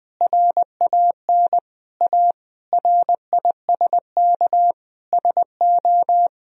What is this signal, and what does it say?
Morse code, heard as a keyed tone, about 20 wpm, 700 Hz. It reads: RAN A RISK SO
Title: Sentences from Top 1000 Words - Review Part 4 - 20wpm